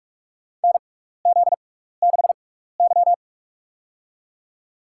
Morse code, heard as a keyed tone, about 43 wpm, spaced as for 15 wpm, 700 Hz. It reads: NZ6Y
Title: USA Call Signs - 4 Characters - 6x Character Spacing - 45wpm